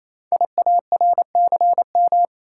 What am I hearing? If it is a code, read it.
Morse code, heard as a keyed tone, about 27 wpm, 700 Hz. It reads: IARCM
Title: Code Groups - 5 Letters - 28wpm